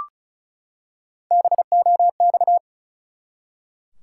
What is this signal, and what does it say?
Morse code, heard as a keyed tone, about 34 wpm, 700 Hz. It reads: BOX
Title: Three Letter Words - 35wpm